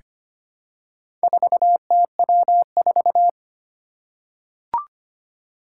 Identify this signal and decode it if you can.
Morse code, heard as a keyed tone, about 25 wpm, 700 Hz. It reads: 4TW4
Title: Four Challenging Letter-Number Combinations 25wpm